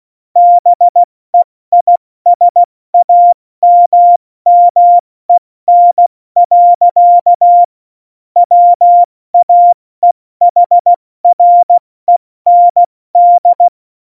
Morse code, heard as a keyed tone, about 16 wpm, spaced as for 14 wpm, 700 Hz. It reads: BEISAMMEN. WAEHREND
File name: Morsekurs / Witze / Witz 08. Oktober 2019, 70 F80